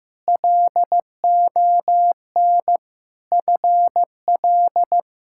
Morse code, heard as a keyed tone, about 15 wpm, 700 Hz. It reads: LON FL
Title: K1USN SST QSO Practice - 15wpm